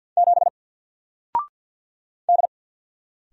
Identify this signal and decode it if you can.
Morse code, heard as a keyed tone, about 48 wpm, 700 Hz. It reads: 7 D
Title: Single Letter-Number 50wpm